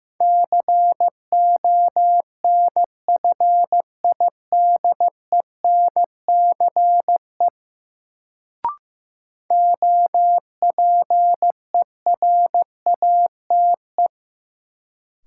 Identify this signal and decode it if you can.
Morse code, heard as a keyed tone, about 15 wpm, 700 Hz. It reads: CONFIDENCE OPERATE
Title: Sentences from Top 1500 Words - New Word Drill - 15wpm